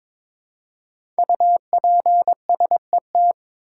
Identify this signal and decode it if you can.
Morse code, heard as a keyed tone, about 22 wpm, 700 Hz. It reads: UPSET
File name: 5 Letter Words - First 3 Letters Given - 22wpm